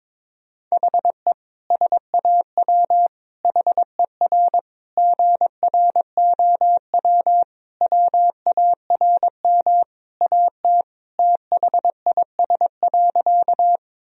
Morse code, heard as a keyed tone, about 21 wpm, 700 Hz. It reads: HE SAW HER GROW WARM AT THIS.
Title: Sentences from Top 1000 Words - Review Part 2 - 22wpm